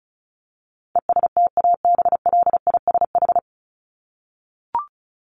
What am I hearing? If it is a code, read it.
Morse code, heard as a keyed tone, about 34 wpm, 700 Hz. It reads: ESTABLISH
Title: ED Suffix - Top 100 Words with Base Words 35wpm